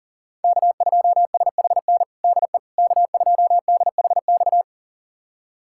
Morse code, heard as a keyed tone, about 40 wpm, 700 Hz. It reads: K2SHN DE K2DHX
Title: USA Callsign DE Callsign - 5 Characters - 40wpm